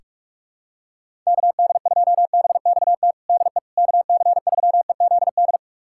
Morse code, heard as a keyed tone, about 42 wpm, 700 Hz. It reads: KD2BXT DE KC3EZD
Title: USA Callsign DE Callsign 45wpm